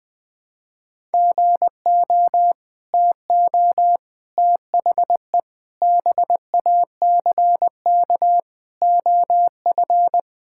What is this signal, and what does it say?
Morse code, heard as a keyed tone, about 19 wpm, 700 Hz. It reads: GO TO THE BACK OF
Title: Sentences with 5-Word Phrases - Spoken Beginning - 20wpm